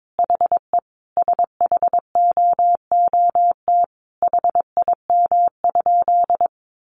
Morse code, heard as a keyed tone, about 22 wpm, 700 Hz. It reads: HE SHOOT HIM?
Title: Sentences from Top 1200 Words 22wpm